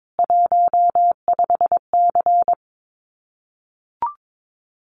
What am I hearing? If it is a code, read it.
Morse code, heard as a keyed tone, about 22 wpm, 700 Hz. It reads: 15C
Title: Common QSO Phrases 22wpm